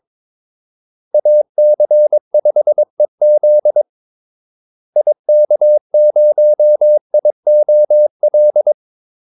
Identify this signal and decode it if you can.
Morse code, heard as a keyed tone, about 22 wpm, 600 Hz. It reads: AC5EZ IK0IOL
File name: Callsigns-22wpm